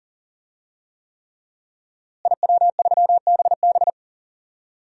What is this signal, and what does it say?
Morse code, heard as a keyed tone, about 40 wpm, 700 Hz. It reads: IW3BB
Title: DXCC-100 Call Signs - 5 Characters - Repeated 3x - 40wpm